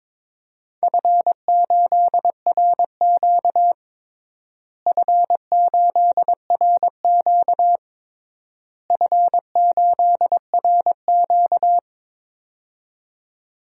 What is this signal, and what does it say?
Morse code, heard as a keyed tone, about 21 wpm, 700 Hz. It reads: F8RQ F8RQ F8RQ
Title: DXCC-100 Call Signs - 4 Characters - Repeated 3x - 22wpm